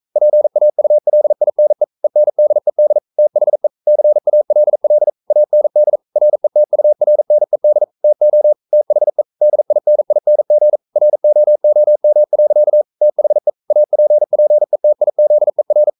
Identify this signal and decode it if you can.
Morse code, heard as a keyed tone, about 42 wpm, 600 Hz. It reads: PAULINE ENDED THE CALL AND RETURNED TO THE DINING ROOM. THE APPETIZER
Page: Romanen Never af Ken Follet på engelsk.